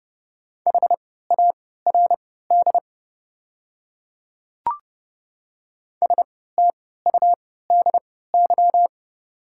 Morse code, heard as a keyed tone, about 30 wpm, spaced as for 17 wpm, 700 Hz. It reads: HARD STUDY
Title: 3x Character Spacing - Top 300 Words - 30wpm